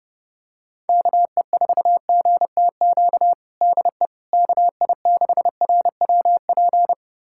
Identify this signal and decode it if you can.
Morse code, heard as a keyed tone, about 32 wpm, 700 Hz. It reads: KE4GTQ DE KI6RWP
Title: USA Callsign DE Callsign - 6 Characters - 30wpm